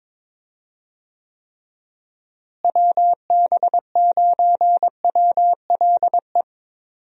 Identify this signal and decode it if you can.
Morse code, heard as a keyed tone, about 22 wpm, 700 Hz. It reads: WB9WLE
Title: USA Call Signs - 6 Characters - Repeated 3x - 22wpm